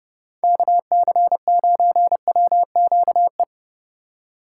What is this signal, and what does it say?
Morse code, heard as a keyed tone, about 28 wpm, 700 Hz. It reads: KC9WQE